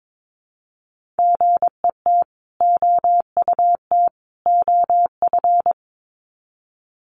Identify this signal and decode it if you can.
Morse code, heard as a keyed tone, about 22 wpm, 700 Hz. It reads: GET OUT OF
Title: Sets of 3 Words 22wpm